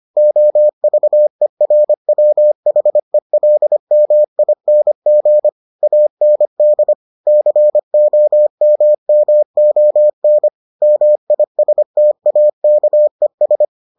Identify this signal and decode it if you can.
Morse code, heard as a keyed tone, about 25 wpm, 600 Hz. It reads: OVERWHELMING AND COMMON MISTAKES